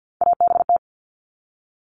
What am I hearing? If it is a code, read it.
Morse code, heard as a keyed tone, about 48 wpm, 700 Hz. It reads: ABT
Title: Abbreviations Repeated 3x - 50wpm